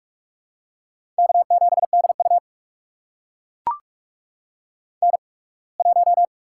Morse code, heard as a keyed tone, about 44 wpm, spaced as for 27 wpm, 700 Hz. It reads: K7DU N1
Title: USA Call Signs - 4 Characters - 8x Character Spacing - 45wpm